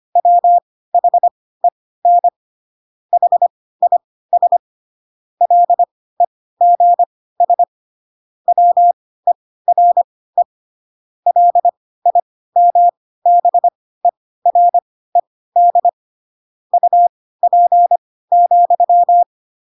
Morse code, heard as a keyed tone, about 25 wpm, spaced as for 16 wpm, 700 Hz. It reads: WHEN HIS LEGS WERE LIMBERED UP,
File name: Pinocchio / Pinocchio ch3 10 25 ik1hge